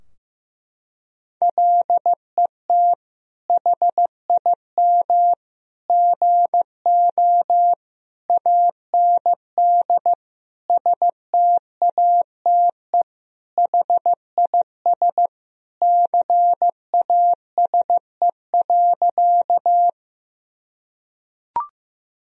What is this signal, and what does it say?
Morse code, heard as a keyed tone, about 15 wpm, 700 Hz. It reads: LET HIM GO AND STATE HIS CASE.
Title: Sentences from Top 1000 Words - Review Part 2 - 15wpm